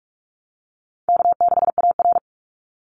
Morse code, heard as a keyed tone, about 44 wpm, 700 Hz. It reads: K6AR
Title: Call Signs - 45wpm